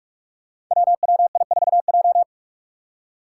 Morse code, heard as a keyed tone, about 46 wpm, 700 Hz. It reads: WWI4J
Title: Five Challenging Letter-Number Combinations 45wpm